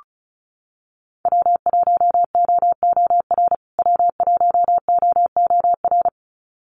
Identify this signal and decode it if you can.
Morse code, heard as a keyed tone, about 34 wpm, 700 Hz. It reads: W1OOR W1OOR
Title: USA Call Signs Repeated 2x 35wpm